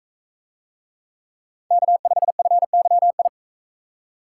Morse code, heard as a keyed tone, about 42 wpm, 700 Hz. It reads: K5FYI